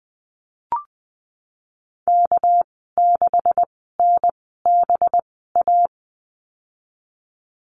Morse code, heard as a keyed tone, about 20 wpm, spaced as for 15 wpm, 700 Hz. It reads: K6NBA
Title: 231 Lesson 37g - Letter x - Call Signs 20wpm